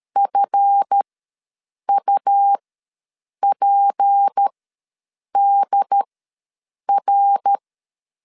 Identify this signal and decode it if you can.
Morse code, heard as a keyed tone, about 13 wpm, spaced as for 7 wpm, 800 Hz. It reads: FUPDR
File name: LEZIONI / 7 GF9Z / 7 GF ws 12wpm 3140